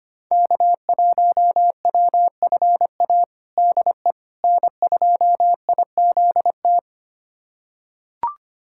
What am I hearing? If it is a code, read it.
Morse code, heard as a keyed tone, about 25 wpm, 700 Hz. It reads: K1WFA DE N2IZT
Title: USA Callsign DE Callsign - 5 Characters - 25wpm